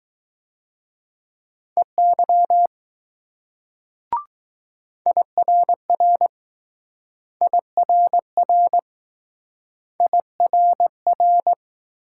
Morse code, heard as a keyed tone, about 20 wpm, 700 Hz. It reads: EY IRR IRR IRR